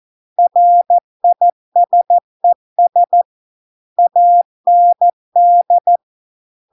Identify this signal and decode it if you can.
Morse code, heard as a keyed tone, about 14 wpm, 700 Hz. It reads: RISES AND